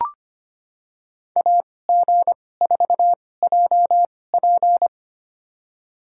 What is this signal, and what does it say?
Morse code, heard as a keyed tone, about 25 wpm, spaced as for 18 wpm, 700 Hz. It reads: AG4JP
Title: USA Call Signs - 5 Characters - 2x Character Spacing - 25wpm